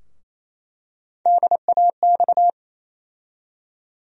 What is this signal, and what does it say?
Morse code, heard as a keyed tone, about 28 wpm, 700 Hz. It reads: DAX
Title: USA Call Sign Postfixes - 3 Letters 28wpm